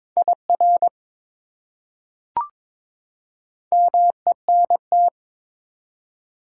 Morse code, heard as a keyed tone, about 21 wpm, 700 Hz. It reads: IR MENT
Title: Prefixes and Suffixes 22wpm